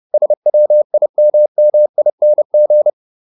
Morse code, heard as a keyed tone, about 30 wpm, 600 Hz. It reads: SWIMMING
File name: ING8 30